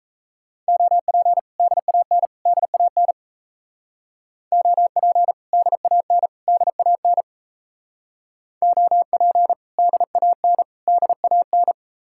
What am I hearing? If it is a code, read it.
Morse code, heard as a keyed tone, about 34 wpm, 700 Hz. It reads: OP DAN DAN OP DAN DAN OP DAN DAN
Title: Speed-Racing Common QSO Phrases - 28wpm